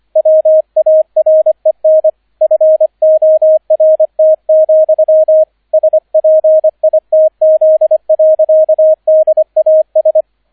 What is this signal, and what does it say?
Morse code, heard as a keyed tone, about 25 wpm, 630 Hz, with radio noise. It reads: WAREN FORT, SPITZ. DAS